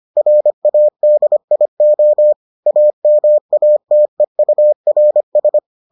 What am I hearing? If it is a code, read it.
Morse code, heard as a keyed tone, about 25 wpm, 600 Hz. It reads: RADIO AMATEURS